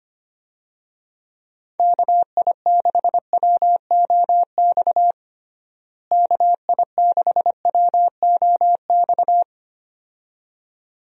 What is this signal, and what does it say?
Morse code, heard as a keyed tone, about 25 wpm, 700 Hz. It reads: KI6WOX KI6WOX
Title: USA Call Signs - 6 Characters - Repeated 2x - 25wpm